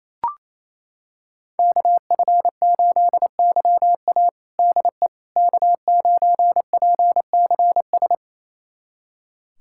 Morse code, heard as a keyed tone, about 28 wpm, 700 Hz. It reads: KF8YA DE K9PCS
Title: USA Callsign DE Callsign - 5 Characters - 28wpm